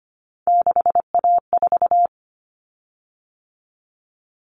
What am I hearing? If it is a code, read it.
Morse code, heard as a keyed tone, about 25 wpm, 700 Hz. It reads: 6A4
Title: Common Cut Numbers - 3-Digit 25wpm